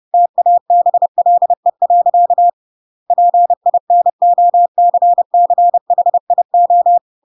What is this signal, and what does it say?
Morse code, heard as a keyed tone, about 30 wpm, 700 Hz. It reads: TABLE. PINOCCHIO
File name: Pinocchio / Pinocchio ch7 30 30 ik1hge